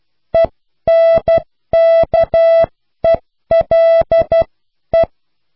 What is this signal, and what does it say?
Morse code, heard as a keyed tone, about 12 wpm, spaced as for 10 wpm, 660 Hz. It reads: ENKELE